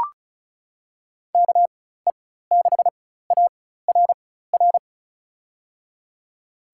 Morse code, heard as a keyed tone, about 36 wpm, spaced as for 16 wpm, 700 Hz. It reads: KE6ARR
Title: USA Call Signs - 6 Characters - 4x Character Spacing - 35wpm